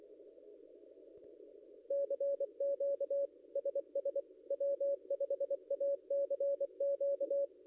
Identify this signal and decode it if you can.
Morse code, heard as a keyed tone, about 24 wpm, 560 Hz. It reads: CQ SS W5ACQ
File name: contests / ss-cw-k5tr-op-n5rz -0800